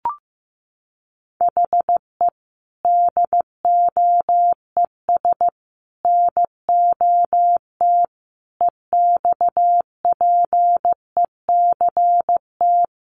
Morse code, heard as a keyed tone, about 15 wpm, 700 Hz. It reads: HE DOES NOT EXPECT
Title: Sentences from Top 1100 Words 15wpm